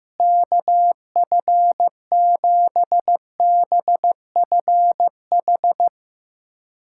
Morse code, heard as a keyed tone, about 15 wpm, 700 Hz. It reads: KF7BFH
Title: USA Callsign DE Callsign 15wpm